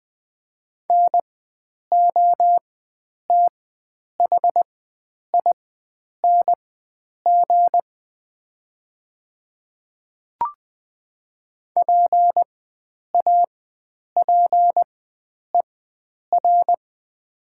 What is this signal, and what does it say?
Morse code, heard as a keyed tone, about 20 wpm, spaced as for 9 wpm, 700 Hz. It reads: NOTHING PAPER
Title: 4x Character Spacing - Top 500 Words - 20wpm